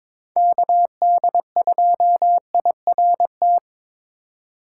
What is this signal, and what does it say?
Morse code, heard as a keyed tone, about 22 wpm, 700 Hz. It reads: KD2IRT
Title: USA Call Signs - 4x Character Spacing - 22wpm